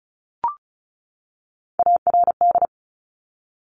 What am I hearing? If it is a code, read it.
Morse code, heard as a keyed tone, about 36 wpm, 700 Hz. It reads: ARD
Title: Top Three Letter Combinations in Words 35wpm